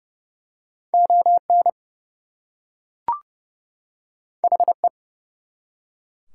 Morse code, heard as a keyed tone, about 29 wpm, 700 Hz. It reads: ON HE
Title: Two Letter Words - 30wpm